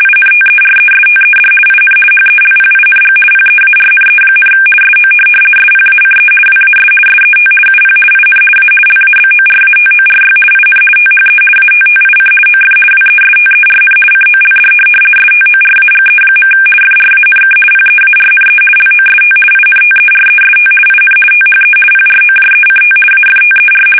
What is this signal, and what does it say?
RTTY, about 50 baud. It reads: INTERESTED IN HELPING PEARNE, SO SPACE WAS SET UP IN A LABORATORY IN THE ATTIC OF WESTERN COLD STORAGE. FRANK PEARNE LOST INTEREST IN THE PROJECT AFTER
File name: fsk850-50B / teletype-corp